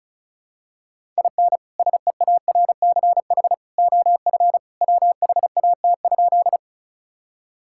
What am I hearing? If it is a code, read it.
Morse code, heard as a keyed tone, about 34 wpm, 700 Hz. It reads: IN SEARCH OF WHAT?
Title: Sentences from Top 1000 Words - Review Part 5 - 35wpm